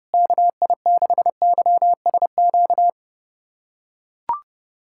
Morse code, heard as a keyed tone, about 30 wpm, 700 Hz. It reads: KI6YSQ
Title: USA Callsign DE Callsign - 6 Characters - 30wpm